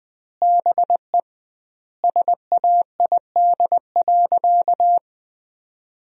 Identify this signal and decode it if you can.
Morse code, heard as a keyed tone, about 20 wpm, 700 Hz. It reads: BE SAID.
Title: 2x Word-Spacing - Sentences from Top 100 Words - 20wpm